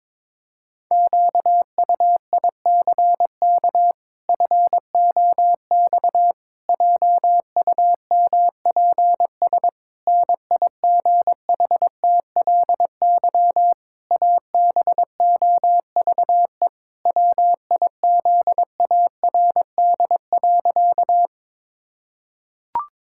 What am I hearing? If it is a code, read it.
Morse code, heard as a keyed tone, about 22 wpm, 700 Hz. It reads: QUICK FOX JUMPS NIGHTLY ABOVE WIZARD.
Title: Pangram Challenge v2 - 22wpm